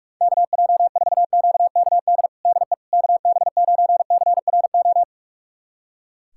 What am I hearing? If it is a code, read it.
Morse code, heard as a keyed tone, about 43 wpm, 700 Hz. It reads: KJ4QKD DE KB9CRO
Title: USA Callsign DE Callsign - 6 Characters - 45wpm